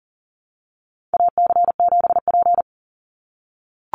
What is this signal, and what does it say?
Morse code, heard as a keyed tone, about 38 wpm, 700 Hz. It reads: AC7P
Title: NA SOTA Chaser Callsigns 40wpm